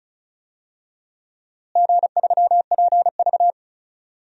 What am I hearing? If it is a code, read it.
Morse code, heard as a keyed tone, about 34 wpm, 700 Hz. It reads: G3PV
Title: DXCC-100 Call Signs - 4 Characters - Repeated 3x - 35wpm